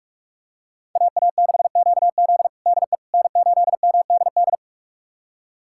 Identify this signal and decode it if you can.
Morse code, heard as a keyed tone, about 41 wpm, 700 Hz. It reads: AA6QZ DE N8MDD
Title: USA Callsign DE Callsign - 5 Characters - 45wpm